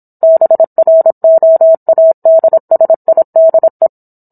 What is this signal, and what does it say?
Morse code, heard as a keyed tone, about 27 wpm, 650 Hz. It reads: BROADSIDE